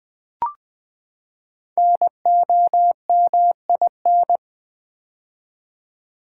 Morse code, heard as a keyed tone, about 20 wpm, 700 Hz. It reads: NOMIN
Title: Build A Word - 5 Letters At A Time - 10-Letter Words - 20wpm